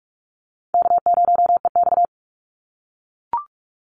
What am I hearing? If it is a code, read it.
Morse code, heard as a keyed tone, about 44 wpm, 700 Hz. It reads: K0EX
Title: Call Signs - 45wpm